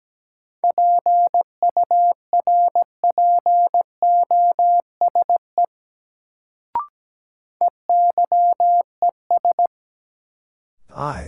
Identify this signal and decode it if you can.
Morse code, heard as a keyed tone, about 17 wpm, 700 Hz. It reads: PURPOSE EYES
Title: Sets of 5 Words - New Word Drill - 17wpm